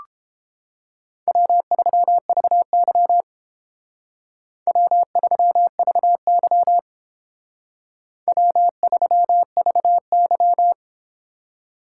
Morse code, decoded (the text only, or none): W3VY W3VY W3VY